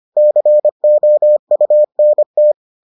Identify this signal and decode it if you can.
Morse code, heard as a keyed tone, about 25 wpm, 600 Hz. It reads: COUNT